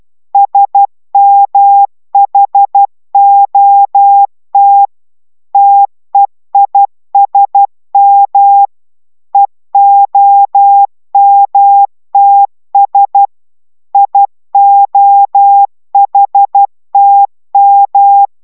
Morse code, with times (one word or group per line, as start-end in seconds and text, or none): 0.3-4.9 SMHOT
5.5-8.7 TEISM
9.3-13.3 EOMTS
13.9-18.3 IOHTM